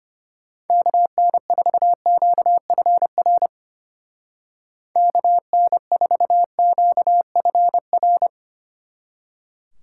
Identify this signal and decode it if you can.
Morse code, heard as a keyed tone, about 33 wpm, 700 Hz. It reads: KN4QFR KN4QFR